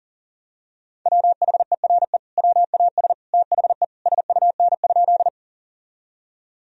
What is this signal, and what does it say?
Morse code, heard as a keyed tone, about 37 wpm, 700 Hz. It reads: WHERE WAS THE SUN?
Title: Sentences from Top 1000 Words - Review Part 1 - 40wpm